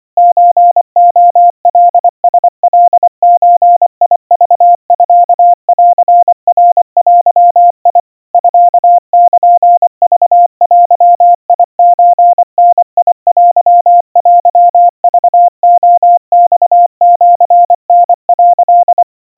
Morse code (Japanese Wabun code), heard as a keyed tone, about 24 wpm, 700 Hz. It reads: ソレカラカゾクミンナデ ミルクデソダテテクレマシタ」
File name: 90 cpm 700 Hz Crispy